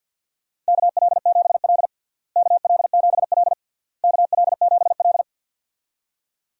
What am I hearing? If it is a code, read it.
Morse code, heard as a keyed tone, about 52 wpm, 700 Hz. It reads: KL7L KL7L KL7L
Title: DXCC-100 Call Signs - 4 Characters - Repeated 3x - 50wpm